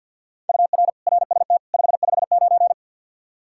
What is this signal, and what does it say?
Morse code, heard as a keyed tone, about 52 wpm, 700 Hz. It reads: UR RST 559
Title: Practice QSO - Exchange 1 - Beginning - 50wpm